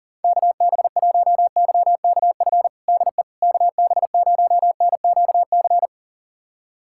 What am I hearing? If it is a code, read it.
Morse code, heard as a keyed tone, about 40 wpm, 700 Hz. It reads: KB1YKF DE KB0NQC
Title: USA Callsign DE Callsign - 6 Characters - 40wpm